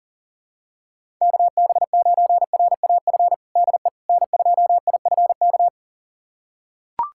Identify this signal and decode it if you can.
Morse code, heard as a keyed tone, about 37 wpm, 700 Hz. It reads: KB9RAF DE N2IFK
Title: USA Callsign DE Callsign 40wpm